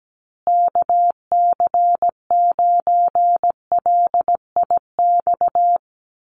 Morse code, heard as a keyed tone, about 17 wpm, 700 Hz. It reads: KC9LIX
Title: USA Call Signs Repeated 3x 17wpm